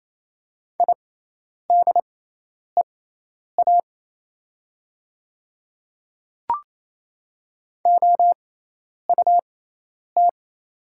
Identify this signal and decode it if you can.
Morse code, heard as a keyed tone, about 28 wpm, spaced as for 10 wpm, 690 Hz. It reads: IDEA OUT
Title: 6x Character Spacing - Top 500 Words - 28wpm